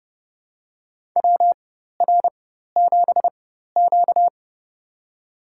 Morse code, heard as a keyed tone, about 30 wpm, spaced as for 14 wpm, 700 Hz. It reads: WR7Q
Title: USA Call Signs - 4 Characters - 4x Character Spacing - 30wpm